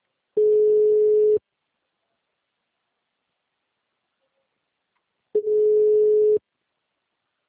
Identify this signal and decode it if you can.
Ringback tone (Europe, 425 Hz).
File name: out- -2016- - - .5101